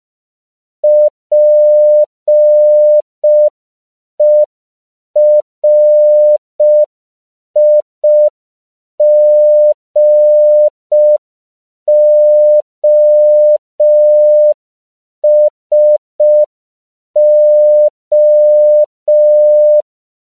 Morse code, heard as a keyed tone, about 5 wpm, 600 Hz. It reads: PERIGOSO
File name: CW / 5PPM